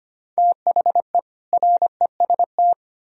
Morse code, heard as a keyed tone, about 33 wpm, 700 Hz. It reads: THE REST